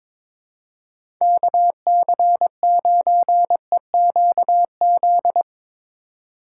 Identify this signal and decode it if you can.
Morse code, heard as a keyed tone, about 22 wpm, 700 Hz. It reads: KC9EQZ
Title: USA Call Signs - 6 Characters - 3x Character Spacing - 22wpm